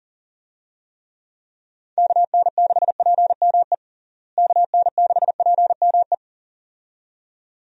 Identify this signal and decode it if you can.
Morse code, heard as a keyed tone, about 37 wpm, 700 Hz. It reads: KN6PME KN6PME
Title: USA Call Signs - 6 Characters - Repeated 2x - 40wpm